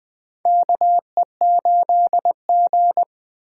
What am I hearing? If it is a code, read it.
Morse code, heard as a keyed tone, about 20 wpm, 700 Hz. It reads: KE8G
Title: USA Call Signs - 4 Characters - 2x Character Spacing - 20wpm